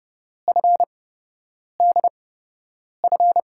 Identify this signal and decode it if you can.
Morse code, heard as a keyed tone, about 32 wpm, spaced as for 8 wpm, 700 Hz. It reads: FDF
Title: USA Call Signs - 8x Character Spacing - 30wpm